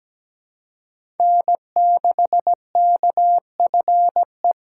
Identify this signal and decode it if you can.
Morse code, heard as a keyed tone, about 17 wpm, 700 Hz. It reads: N6KFE
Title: USA Call Signs - 6x Character Spacing - 17wpm